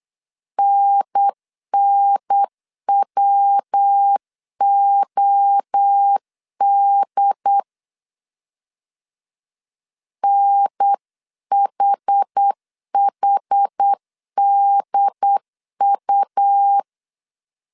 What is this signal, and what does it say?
Morse code, heard as a keyed tone, about 8 wpm, 800 Hz. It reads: NNWOD NHHDU